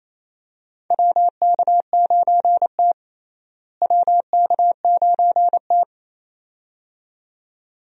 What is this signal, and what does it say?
Morse code, heard as a keyed tone, about 27 wpm, 700 Hz. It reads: WK9T WK9T
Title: USA Call Signs - 4 Characters - Repeated 2x - 28wpm